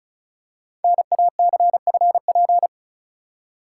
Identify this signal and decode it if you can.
Morse code, heard as a keyed tone, about 36 wpm, 700 Hz. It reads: NACFP
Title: Code Groups - 5 Letter:Numbers - 35wpm